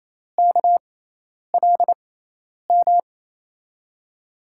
Morse code, heard as a keyed tone, about 27 wpm, spaced as for 10 wpm, 700 Hz. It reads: KLM
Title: USA Call Signs - 5 Characters - 6x Character Spacing - 28wpm